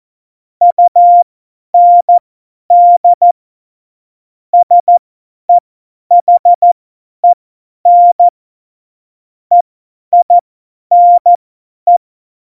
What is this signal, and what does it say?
Morse code, heard as a keyed tone, about 14 wpm, spaced as for 10 wpm, 700 Hz. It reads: UND SEHEN EINE
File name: Morsekurs / Witze / Witz 11. September 2018, 50 F70